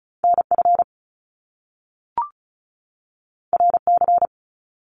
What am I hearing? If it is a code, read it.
Morse code, heard as a keyed tone, about 34 wpm, 700 Hz. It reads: NF RC
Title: Top Two Letter Combinations in Words 35wpm